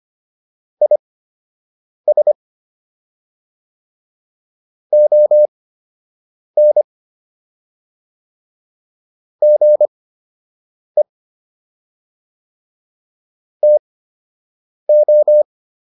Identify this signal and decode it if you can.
Morse code, heard as a keyed tone, about 25 wpm, spaced as for 7 wpm, 600 Hz. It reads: IS ON GE TO